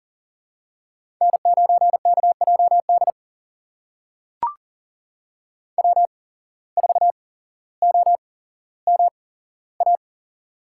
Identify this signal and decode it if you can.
Morse code, heard as a keyed tone, about 40 wpm, spaced as for 17 wpm, 700 Hz. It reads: N9KJD W4OMA
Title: USA Call Signs - 5 Characters - 8x Character Spacing - 40wpm